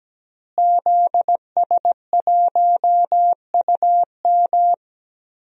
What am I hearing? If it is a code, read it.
Morse code, heard as a keyed tone, about 17 wpm, 700 Hz. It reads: ZS1UM
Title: DXCC-100 Call Signs - 5 Characters - Repeated 3x - 17wpm